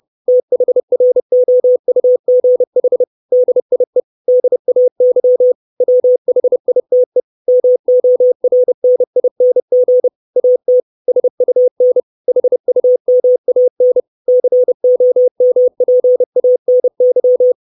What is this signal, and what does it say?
Morse code, heard as a keyed tone, about 30 wpm, 500 Hz. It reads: THROUGH DIE DAY WHITE MORNING AT SUN HUMAN COMPANY
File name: 632 Words 30WPM